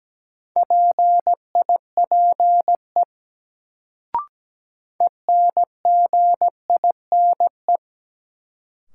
Morse code, heard as a keyed tone, about 17 wpm, 700 Hz. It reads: PIPE ENGINE
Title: Sentences from Top 1500 Words - New Word Drill Alt - 17wpm